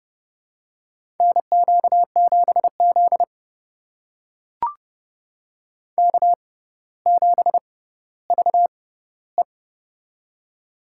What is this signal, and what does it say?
Morse code, heard as a keyed tone, about 30 wpm, spaced as for 15 wpm, 700 Hz. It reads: NQ7Z K7VE
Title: USA Call Signs - 4 Characters - 6x Character Spacing - 30wpm